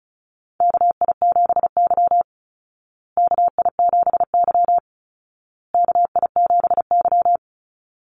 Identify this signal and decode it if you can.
Morse code, heard as a keyed tone, about 34 wpm, 700 Hz. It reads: KI7Y KI7Y KI7Y
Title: Call Signs Repeated 2x - 35wpm